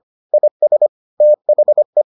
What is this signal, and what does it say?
Morse code, heard as a keyed tone, about 25 wpm, 600 Hz. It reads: IS THE